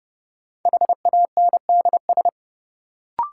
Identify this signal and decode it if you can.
Morse code, heard as a keyed tone, about 30 wpm, 700 Hz. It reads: HANDS
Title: Sentences from Top 1000 Words - Review Part 6 - New Word Drill - 30wpm